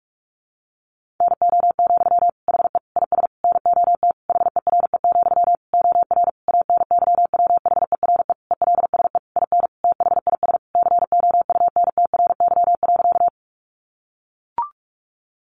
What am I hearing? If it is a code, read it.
Morse code, heard as a keyed tone, about 44 wpm, 700 Hz. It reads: NO, HE IS NOT HERE, OR ANYWHERE ELSE IN THIS COUNTRY.
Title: Sentences from Top 1400 Words 45wpm